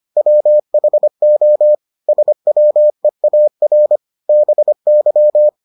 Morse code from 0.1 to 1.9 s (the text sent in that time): WHO